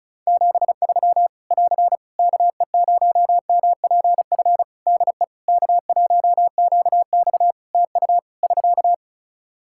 Morse code, heard as a keyed tone, about 36 wpm, 700 Hz. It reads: 73 <AR> KE0MPF DE K1QX TU <SK>
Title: Practice QSO - Exchange 4 - Ending - 35wpm